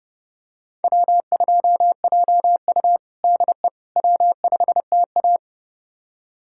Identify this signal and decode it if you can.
Morse code, heard as a keyed tone, about 30 wpm, 700 Hz. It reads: W2JU DE W5TA
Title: USA Callsign DE Callsign - 4 Characters - 30wpm